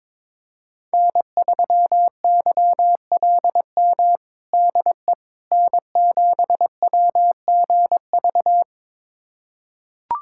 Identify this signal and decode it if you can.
Morse code, heard as a keyed tone, about 22 wpm, 700 Hz. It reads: N3YLM DE N7WGV
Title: USA Callsign DE Callsign - 5 Characters - 22wpm